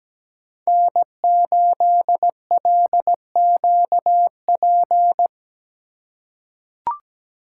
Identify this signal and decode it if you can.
Morse code, heard as a keyed tone, about 17 wpm, 700 Hz. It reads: N8LQP
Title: USA Call Signs - 5 Characters - 4x Character Spacing - 17wpm